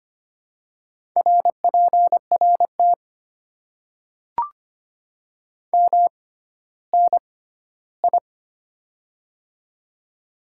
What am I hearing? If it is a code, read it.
Morse code, heard as a keyed tone, about 25 wpm, spaced as for 14 wpm, 700 Hz. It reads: RPRT MNI
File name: Top 90 QSO Elements - 6x Character Spacing - 25wpm